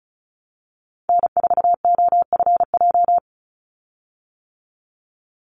Morse code, heard as a keyed tone, about 34 wpm, 700 Hz. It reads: N4OFJ
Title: USA Call Signs 35wpm